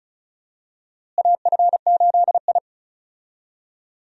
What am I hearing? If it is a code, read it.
Morse code, heard as a keyed tone, about 36 wpm, 700 Hz. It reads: AF8I
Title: USA Call Signs - 4 Characters - Repeated 3x - 35wpm